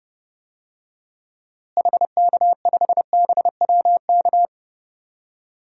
Morse code, heard as a keyed tone, about 30 wpm, 700 Hz. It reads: HK5BWK
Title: DXCC-100 Call Signs - 6 Characters - Repeated 3x - 30wpm